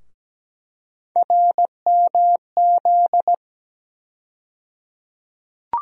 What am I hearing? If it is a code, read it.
Morse code, heard as a keyed tone, about 17 wpm, 700 Hz. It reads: RMZ
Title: USA Call Sign Postfixes - 3 Letters 17wpm